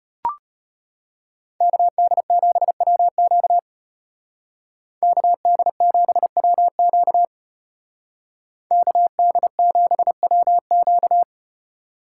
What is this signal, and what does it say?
Morse code, heard as a keyed tone, about 33 wpm, 700 Hz. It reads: KD7WQ KD7WQ KD7WQ